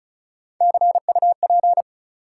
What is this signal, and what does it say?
Morse code, heard as a keyed tone, about 34 wpm, 700 Hz. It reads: CUP
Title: Sentences from Top 1600 Words - New Word Drill - 35wpm